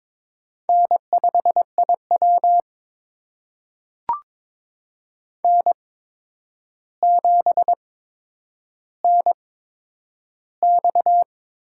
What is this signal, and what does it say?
Morse code, heard as a keyed tone, about 22 wpm, spaced as for 9 wpm, 700 Hz. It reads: N5IW N7NX
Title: USA Call Signs - 4 Characters - 8x Character Spacing - 22wpm